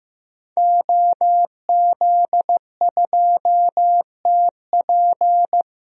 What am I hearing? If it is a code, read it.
Morse code, heard as a keyed tone, about 16 wpm, 700 Hz. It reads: OZ2TP
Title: DXCC-100 Call Signs - 5 Characters - 15wpm